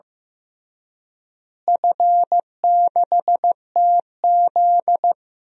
Morse code, heard as a keyed tone, about 15 wpm, 700 Hz. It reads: F6TZ
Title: DXCC-100 Call Signs - 4 Characters - Repeated 2x - 15wpm